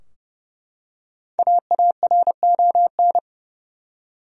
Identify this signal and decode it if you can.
Morse code, heard as a keyed tone, about 29 wpm, 700 Hz. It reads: AARON
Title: USA Common Names - 30wpm